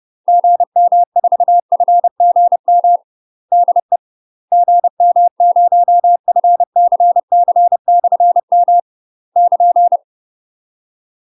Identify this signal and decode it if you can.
Morse code, heard as a keyed tone, about 30 wpm, 700 Hz. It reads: GM4FGM DE GM0FCC/M <KN>
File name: UK MASTER QSOs TRAINING / 30WPM / UK MASTER QSOs 0034